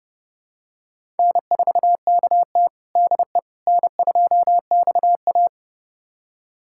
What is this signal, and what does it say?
Morse code, heard as a keyed tone, about 30 wpm, 700 Hz. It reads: N4KT DE N2XA
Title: USA Callsign DE Callsign - 4 Characters - 30wpm